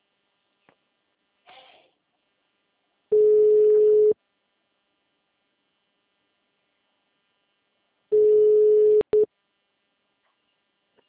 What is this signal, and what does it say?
Ringback tone (Europe, 425 Hz).
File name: out- -2005- - - .59447